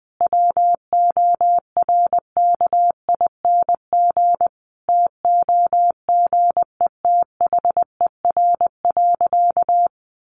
Morse code, heard as a keyed tone, about 20 wpm, 700 Hz. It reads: WORKING TOGETHER.
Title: Sentences from Top 1200 Words 20wpm